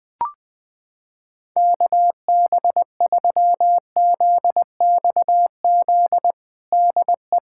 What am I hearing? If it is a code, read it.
Morse code, heard as a keyed tone, about 21 wpm, 700 Hz. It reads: KB3ZXZ DE
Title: USA Callsign DE Callsign - 6 Characters - 20wpm